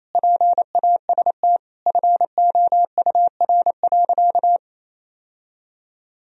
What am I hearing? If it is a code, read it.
Morse code, heard as a keyed tone, about 28 wpm, 700 Hz. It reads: PAST FOUR.
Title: Sentences from Top 1000 Words - Review Part 3 - 28wpm